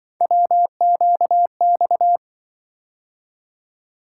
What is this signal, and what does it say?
Morse code, heard as a keyed tone, about 24 wpm, 700 Hz. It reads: WQX
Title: Speed-Racing - Three Letter-Number Combinations - 20wpm